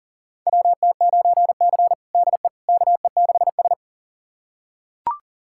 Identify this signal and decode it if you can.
Morse code, heard as a keyed tone, about 40 wpm, 700 Hz. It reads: WT9C DE KE6S
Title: USA Callsign DE Callsign - 4 Characters - 40wpm